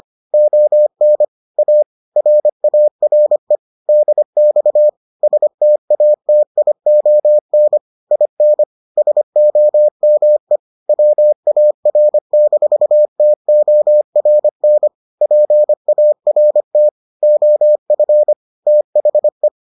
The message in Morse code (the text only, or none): ON A RARE DX STATION IN SOME WAR-TORN PART OF THE